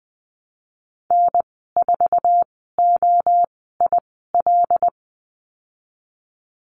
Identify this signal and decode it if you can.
Morse code, heard as a keyed tone, about 20 wpm, spaced as for 15 wpm, 700 Hz. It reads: N4OIL
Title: 167 Lesson 27g - Number 7 - Call Signs 20wpm